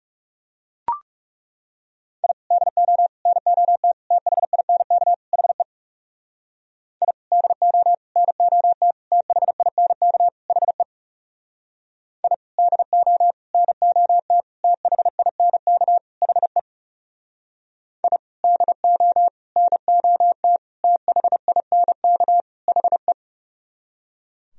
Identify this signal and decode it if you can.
Morse code, heard as a keyed tone, about 37 wpm, 700 Hz. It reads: I DO NOT THINK HE I DO NOT THINK HE I DO NOT THINK HE I DO NOT THINK HE
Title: Speed-Racing - Sets of 5 Words 30wpm